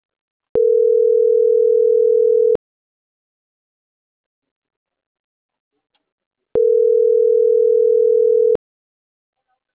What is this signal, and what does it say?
Ringback tone (North America), 440 + 480 Hz.